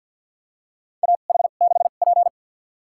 Morse code, heard as a keyed tone, about 53 wpm, spaced as for 37 wpm, 700 Hz. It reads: AH6P
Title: USA Call Signs - 2x Character Spacing - 50wpm